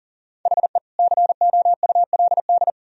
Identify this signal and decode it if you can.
Morse code, heard as a keyed tone, about 38 wpm, 700 Hz. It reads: HE COULD
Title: Sentences from Top 300 Words and Abbreviations 40wpm